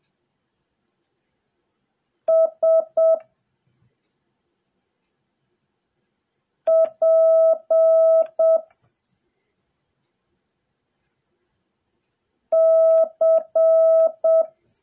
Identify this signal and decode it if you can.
Morse code, heard as a keyed tone, about 7 wpm, 660 Hz. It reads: S P C
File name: jh8cbh / 2021cw / cw08a